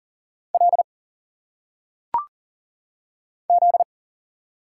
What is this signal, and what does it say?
Morse code, heard as a keyed tone, about 40 wpm, 700 Hz. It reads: L Z
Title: Single Letters - Phonetic Pronunciation 40wpm